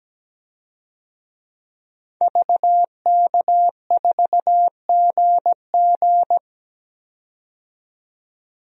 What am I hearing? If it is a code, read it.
Morse code, heard as a keyed tone, about 17 wpm, 700 Hz. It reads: VK4GG
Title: DXCC-100 Call Signs - 5 Characters - 17wpm